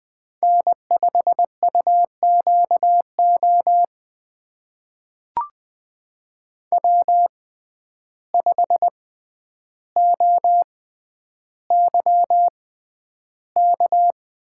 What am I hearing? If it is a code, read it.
Morse code, heard as a keyed tone, about 20 wpm, spaced as for 10 wpm, 700 Hz. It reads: N5UQO W5OYK
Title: USA Call Signs - 5 Characters - 6x Character Spacing - 20wpm